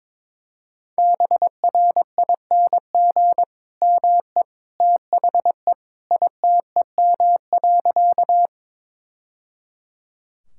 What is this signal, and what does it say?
Morse code, heard as a keyed tone, about 21 wpm, 700 Hz. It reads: BRING ME THE ITEM.
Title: Sentences from Top 1000 Words - Review Part 6 - 22wpm